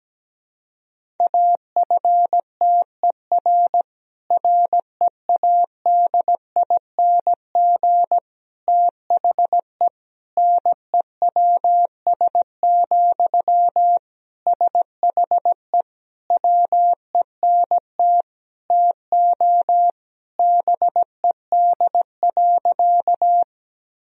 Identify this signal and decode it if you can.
Morse code, heard as a keyed tone, about 17 wpm, 700 Hz. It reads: AFTER READING THE NEWS, SHE WENT TO BED.
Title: Sentences from Top 1000 Words - Review Part 3 - 17wpm